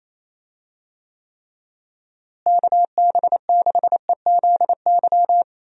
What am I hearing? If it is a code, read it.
Morse code, heard as a keyed tone, about 28 wpm, 700 Hz. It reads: KB6EZY
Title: USA Call Signs - 6 Characters - Repeated 3x - 28wpm